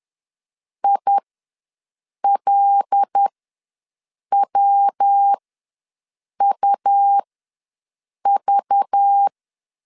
Morse code, heard as a keyed tone, about 10 wpm, spaced as for 6 wpm, 800 Hz. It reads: ILWUV
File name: LEZIONI / 5 LBVC / 5 LBV ws 10wpm 3645